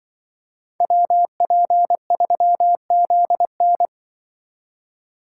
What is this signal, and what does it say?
Morse code, heard as a keyed tone, about 24 wpm, 700 Hz. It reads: WP3ZN
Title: Speed Racing - USA Call Signs 20wpm